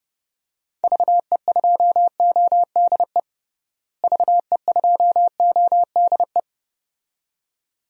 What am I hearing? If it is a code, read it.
Morse code, heard as a keyed tone, about 30 wpm, 700 Hz. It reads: VE2ODE VE2ODE
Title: DXCC-100 Call Signs - 6 Characters - Repeated 2x - 30wpm